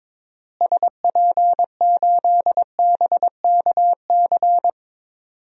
Morse code, heard as a keyed tone, about 21 wpm, 700 Hz. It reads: SP8BKC
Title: DXCC-100 Call Signs - 6 Characters - Repeated 2x - 22wpm